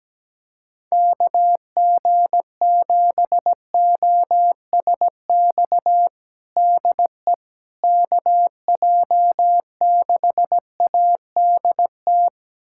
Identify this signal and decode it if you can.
Morse code, heard as a keyed tone, about 17 wpm, 700 Hz. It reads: KG7OSX DE KJ6ADT
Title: USA Callsign DE Callsign 17wpm